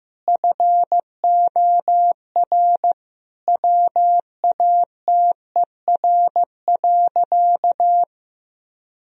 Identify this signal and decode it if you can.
Morse code, heard as a keyed tone, about 15 wpm, 700 Hz. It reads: FOR WATER.
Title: Sentences from Top 1000 Words - Review Part 4 - 15wpm